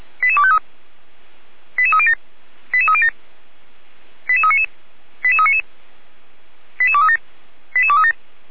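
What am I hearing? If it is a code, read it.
Five-tone selective call (ZVEI), heard as quick runs of five tones, tones of about 70 ms. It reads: 88253 88307 88307 88399 88399 88237 88237
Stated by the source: ZVEI five-tone selective call received with an SDR, from feuerwehr-orlamuende (fire brigade Orlamünde).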